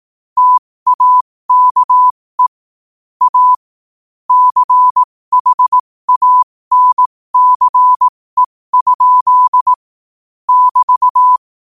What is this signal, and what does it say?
Morse code, heard as a keyed tone, about 18 wpm, spaced as for 15 wpm, 1000 Hz. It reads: TAKE A CHANCE? <BT>
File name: qotdfiles / 15wpm